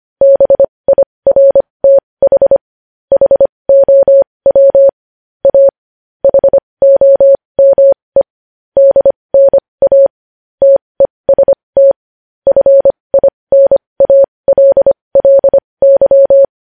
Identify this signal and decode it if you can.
Morse code, heard as a keyed tone, about 25 wpm, spaced as for 20 wpm, 550 Hz. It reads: BIRTH HOW A HOME DNA TEST FINALLY
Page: News headlines in morse code at 20 wpm